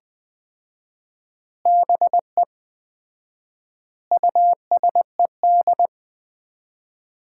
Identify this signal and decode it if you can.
Morse code, heard as a keyed tone, about 20 wpm, 700 Hz. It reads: BE USED
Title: 4x Word-Spacing - Sets of 4 Words - 20wpm